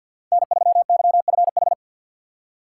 Morse code, heard as a keyed tone, about 52 wpm, 700 Hz. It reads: N3YFH
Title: USA Call Signs - 5 Characters - 2x Character Spacing - 50wpm